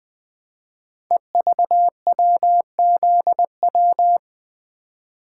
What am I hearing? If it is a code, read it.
Morse code, heard as a keyed tone, about 20 wpm, 700 Hz. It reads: EVWZW
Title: Code Groups - 5 Letter:Numbers - 20wpm